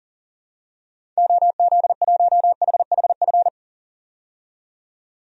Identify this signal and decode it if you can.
Morse code, heard as a keyed tone, about 39 wpm, 700 Hz. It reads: OZ1HHF
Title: DXCC-100 Call Signs - 6 Characters - 40wpm